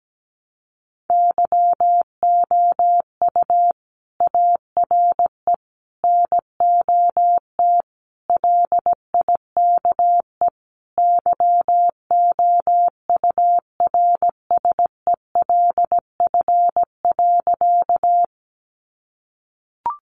Morse code, heard as a keyed tone, about 17 wpm, 700 Hz. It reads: YOU ARE NOT LIKE YOURSELF.
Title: Sentences from Top 600 Words 17wpm